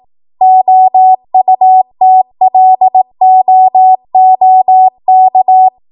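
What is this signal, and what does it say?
Morse code, heard as a keyed tone, about 18 wpm, 750 Hz. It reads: OUTLOOK
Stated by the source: ARRL 18 wpm Morse code practice archive transmission.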